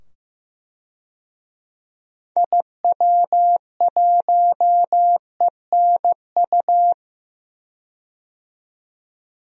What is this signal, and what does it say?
Morse code, heard as a keyed tone, about 15 wpm, 700 Hz. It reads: IW1ENU
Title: DXCC-100 Call Signs - 6 Characters - 15wpm